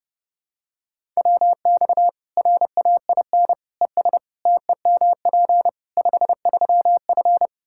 Morse code, heard as a keyed tone, about 32 wpm, 700 Hz. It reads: WX RAIN ES TEMP 53F
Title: Practice QSO - Exchange 2 - Info - 30wpm